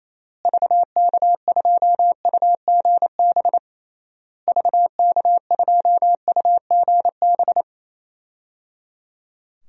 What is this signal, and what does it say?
Morse code, heard as a keyed tone, about 28 wpm, 700 Hz. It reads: VK2UGB VK2UGB
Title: DXCC-100 Call Signs - 6 Characters - Repeated 3x - 28wpm